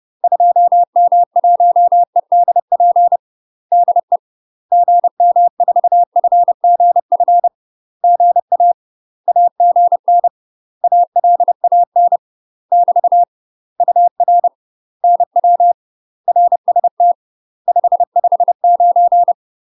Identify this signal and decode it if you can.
Morse code, heard as a keyed tone, about 30 wpm, 700 Hz. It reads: 2M1EDP DE GM4FGF GA AGN ALAN <BT> UR NW RST 559
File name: UK MASTER QSOs TRAINING / 30WPM / UK MASTER QSOs 0049